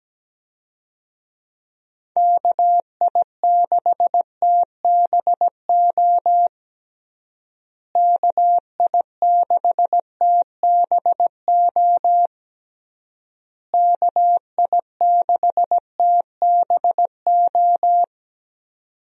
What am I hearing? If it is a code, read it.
Morse code, heard as a keyed tone, about 17 wpm, 700 Hz. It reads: KI6TBO KI6TBO KI6TBO
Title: USA Call Signs - 6 Characters - Repeated 3x - 17wpm